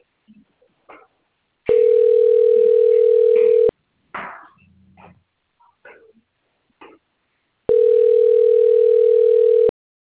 Ringback tone (North America), 440 + 480 Hz.